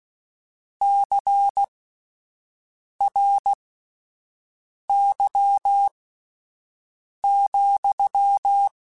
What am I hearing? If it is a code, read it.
Morse code, heard as a keyed tone, about 16 wpm, spaced as for 5 wpm, 800 Hz. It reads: CRY,
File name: Misc / Misc-mp3s / Word Pairs @5